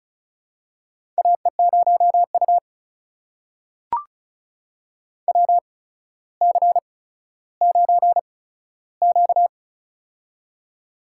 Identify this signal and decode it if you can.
Morse code, heard as a keyed tone, about 34 wpm, spaced as for 15 wpm, 700 Hz. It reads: AE0U WC9Q
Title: USA Call Signs - 4 Characters - 8x Character Spacing - 35wpm